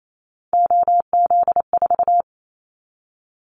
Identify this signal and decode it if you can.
Morse code, heard as a keyed tone, about 28 wpm, 700 Hz. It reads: OZ4
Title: Callsign Prefixes for Top 100 DXCC - 28wpm